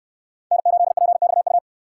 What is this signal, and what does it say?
Morse code, heard as a keyed tone, about 68 wpm, 700 Hz. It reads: N8PXU